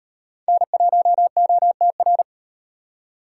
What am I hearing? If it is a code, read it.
Morse code, heard as a keyed tone, about 38 wpm, 700 Hz. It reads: N1OTR